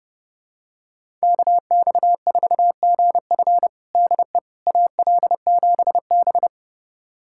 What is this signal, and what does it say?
Morse code, heard as a keyed tone, about 30 wpm, 700 Hz. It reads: KX4GF DE AL7B
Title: USA Callsign DE Callsign 30wpm